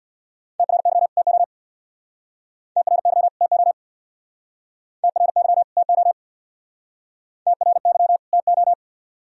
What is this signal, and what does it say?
Morse code, heard as a keyed tone, about 62 wpm, 700 Hz. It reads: TRY TO TRY TO TRY TO TRY TO
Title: Speed-Racing - Sets of 2 Words 50wpm